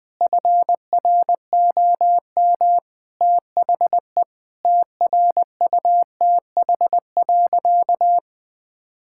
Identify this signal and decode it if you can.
Morse code, heard as a keyed tone, about 20 wpm, 700 Hz. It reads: FROM THE TRUTH.
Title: Sentences with 7-Word Phrases - 20wpm